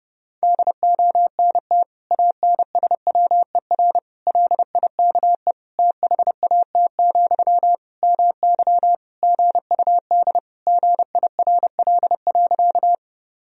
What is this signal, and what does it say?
Morse code, heard as a keyed tone, about 30 wpm, 700 Hz. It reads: DONT ANSWER LIKE THAT, MY GUD GIRL.
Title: Sentences from Top 400 Words and Abbreviations 30wpm